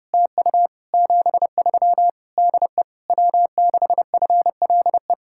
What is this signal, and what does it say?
Morse code, heard as a keyed tone, about 30 wpm, 700 Hz. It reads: TU 73 DE W6FLE
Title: Speed-Racing Common QSO Phrases - 25wpm